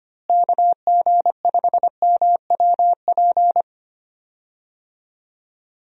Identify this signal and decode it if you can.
Morse code, heard as a keyed tone, about 25 wpm, 700 Hz. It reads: KG5MWP
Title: USA Call Signs - 6 Characters - Repeated 3x - 25wpm